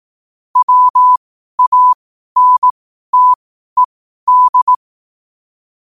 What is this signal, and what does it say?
Morse code, heard as a keyed tone, about 18 wpm, spaced as for 12 wpm, 1000 Hz. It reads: WANTED
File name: qotdfiles / 12wpm